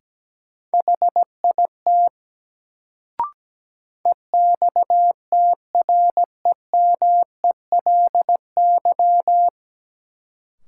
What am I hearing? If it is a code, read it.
Morse code, heard as a keyed tone, about 17 wpm, 700 Hz. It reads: HIT EXTREMELY
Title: Sentences from Top 1200 Words - New Word Drill Alt - 17wpm